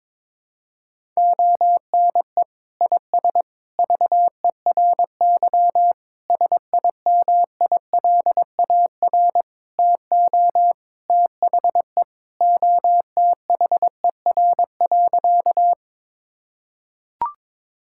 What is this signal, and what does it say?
Morse code, heard as a keyed tone, about 22 wpm, 700 Hz. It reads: ONE IS VERY SIMILAR TO THE OTHER.
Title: Sentences from Top 1000 Words - Review Part 5 - 22wpm